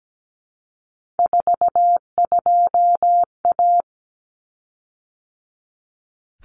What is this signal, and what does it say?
Morse code, heard as a keyed tone, about 17 wpm, 700 Hz. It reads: 42A
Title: Common Cut Numbers - 3-Digit 17wpm